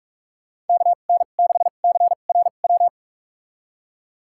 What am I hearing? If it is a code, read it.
Morse code, heard as a keyed tone, about 46 wpm, spaced as for 32 wpm, 700 Hz. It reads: KN6CRW
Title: USA Call Signs - 6 Characters - 2x Character Spacing - 45wpm